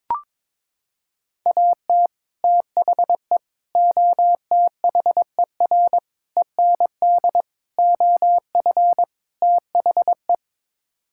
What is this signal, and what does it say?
Morse code, heard as a keyed tone, about 22 wpm, 700 Hz. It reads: AT THE OTHER END OF THE
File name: practice / 22wpm / Sets of 6 Words - 22wpm